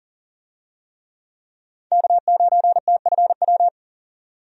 Morse code, heard as a keyed tone, about 40 wpm, 700 Hz. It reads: K9TFW
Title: USA Call Signs - 5 Characters - 40wpm